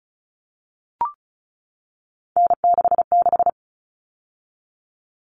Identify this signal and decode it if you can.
Morse code, heard as a keyed tone, about 34 wpm, 700 Hz. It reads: N66
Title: All Cut Numbers - 3-Digit 35wpm